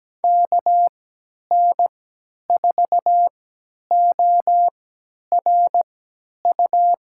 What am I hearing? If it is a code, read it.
Morse code, heard as a keyed tone, about 17 wpm, spaced as for 10 wpm, 700 Hz. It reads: KN4ORU
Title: USA Call Signs - 6 Characters - 3x Character Spacing - 17wpm